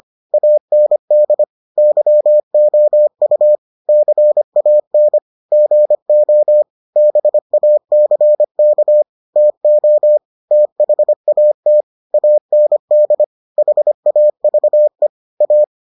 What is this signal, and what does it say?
Morse code, heard as a keyed tone, about 25 wpm, 600 Hz. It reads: AND YOU CAN GO BACK TO THAT AND HAVE A